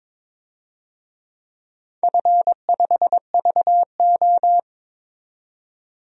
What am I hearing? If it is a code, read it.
Morse code, heard as a keyed tone, about 23 wpm, 700 Hz. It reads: F5VO
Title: DXCC-100 Call Signs - 4 Characters - 22wpm